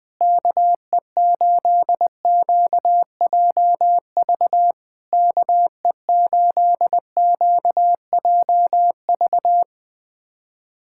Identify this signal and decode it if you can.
Morse code, heard as a keyed tone, about 20 wpm, 690 Hz. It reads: KE8QJV KE8QJV
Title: POTA QSO Practice - 20wpm